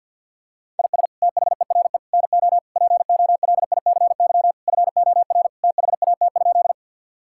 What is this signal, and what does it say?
Morse code, heard as a keyed tone, about 50 wpm, 700 Hz. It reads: IS THERE NO POLICY FOR THAT?
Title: Sentences from Top 1000 Words - Review Part 5 - 50wpm